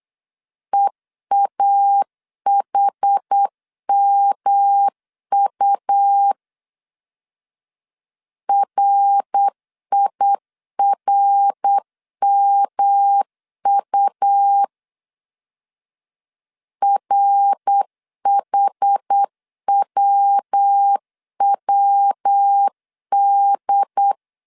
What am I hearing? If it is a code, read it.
Morse code, heard as a keyed tone, about 8 wpm, 800 Hz. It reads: EAHMU RIRMU RHWWD
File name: CW / DET / 1 3 1500C S B 8wpm 4141